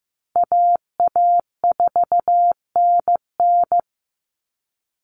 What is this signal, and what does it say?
Morse code, heard as a keyed tone, about 15 wpm, 700 Hz. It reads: AA4NN
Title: Call Signs Repeated 3x - 15wpm